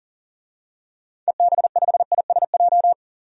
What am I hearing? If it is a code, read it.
Morse code, heard as a keyed tone, about 40 wpm, 700 Hz. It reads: EB5ISJ
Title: DXCC-100 Call Signs - 6 Characters - 40wpm